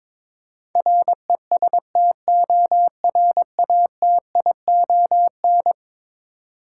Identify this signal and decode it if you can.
Morse code, heard as a keyed tone, about 22 wpm, 700 Hz. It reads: RESTORATION
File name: practice / 22wpm / Eleven Letter Words - 22wpm